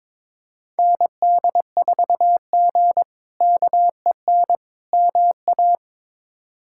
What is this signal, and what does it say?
Morse code, heard as a keyed tone, about 22 wpm, 700 Hz. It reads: ND4G KEN MA
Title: K1USN SST QSO Practice - 22wpm